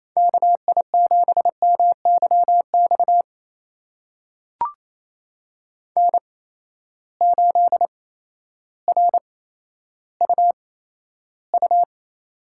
Morse code, heard as a keyed tone, about 28 wpm, spaced as for 13 wpm, 700 Hz. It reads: KI7MYX N8RUU
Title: USA Call Signs - 8x Character Spacing - 28wpm